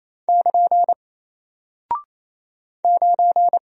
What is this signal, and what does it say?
Morse code, heard as a keyed tone, about 28 wpm, 700 Hz. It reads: <KN> 9
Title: Single Character (Letter:Number:Punctuation:Prosign) Alt 28wpm